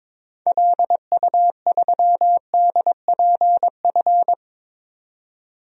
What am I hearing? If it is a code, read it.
Morse code, heard as a keyed tone, about 22 wpm, 700 Hz. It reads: LU3DPF
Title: DXCC-100 Call Signs - 6 Characters - 22wpm